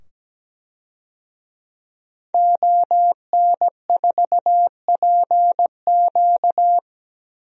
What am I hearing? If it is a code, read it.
Morse code, heard as a keyed tone, about 17 wpm, 700 Hz. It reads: ON4PQ
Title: DXCC-100 Call Signs - 5 Characters - 17wpm